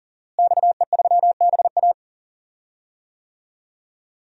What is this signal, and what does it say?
Morse code, heard as a keyed tone, about 40 wpm, 700 Hz. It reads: XE3BA
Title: DXCC-100 Call Signs - 5 Characters - Repeated 3x - 40wpm